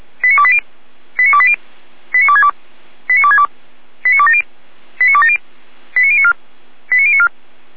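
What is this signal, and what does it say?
Five-tone selective call (ZVEI), heard as quick runs of five tones, tones of about 70 ms. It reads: 89280 89280 89262 89262 89270 89270 89094 89094
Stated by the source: ZVEI five-tone selective call, received on SDR.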